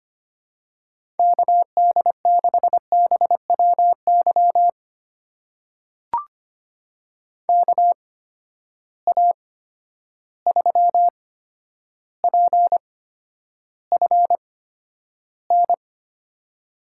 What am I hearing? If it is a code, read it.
Morse code, heard as a keyed tone, about 25 wpm, spaced as for 11 wpm, 700 Hz. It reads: KD6BWY KA3PFN
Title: USA Call Signs - 6 Characters - 8x Character Spacing - 25wpm